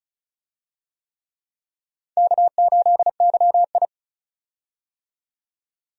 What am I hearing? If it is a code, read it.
Morse code, heard as a keyed tone, about 34 wpm, 700 Hz. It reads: K8YI
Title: USA Call Signs - 4 Characters - 35wpm